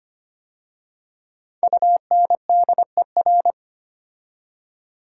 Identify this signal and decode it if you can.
Morse code, heard as a keyed tone, about 25 wpm, 700 Hz. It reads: UNDER
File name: 5 Letter Words - First 3 Letters Given - 25wpm